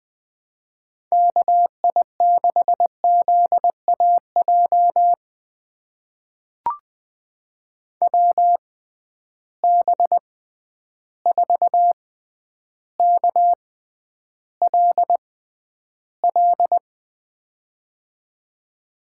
Morse code, heard as a keyed tone, about 20 wpm, spaced as for 10 wpm, 700 Hz. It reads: KI6ZAJ WB4KLL
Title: USA Call Signs - 6 Characters - 6x Character Spacing - 20wpm